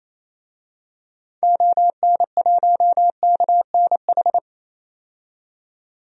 Morse code, heard as a keyed tone, about 28 wpm, 700 Hz. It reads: ON1KNH
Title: DXCC-100 Call Signs - 6 Characters - 28wpm